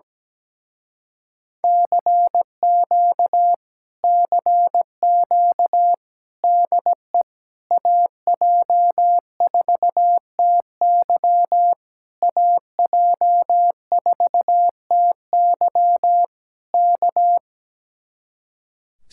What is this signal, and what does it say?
Morse code, heard as a keyed tone, about 17 wpm, 700 Hz. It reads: CQ CQ DE AJ4TY AJ4TY K
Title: Speed-Racing Common QSO Phrases - 17wpm